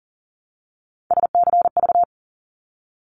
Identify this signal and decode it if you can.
Morse code, heard as a keyed tone, about 38 wpm, 700 Hz. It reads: SCV
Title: ARRL Sections 40wpm